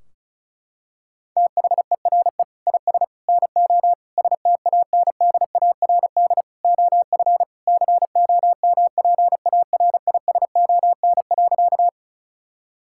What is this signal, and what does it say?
Morse code, heard as a keyed tone, about 34 wpm, 700 Hz. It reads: THERE IS NO STANDARD OF COMPARISON.
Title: Sentences from Top 1600 Words 35wpm